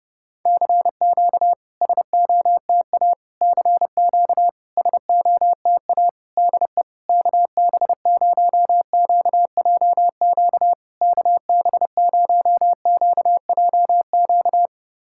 Morse code, heard as a keyed tone, about 30 wpm, 700 Hz. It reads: CQ SOTA CQ SOTA DE KB0QJQ KB0QJQ
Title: SOTA QSO Practice 30wpm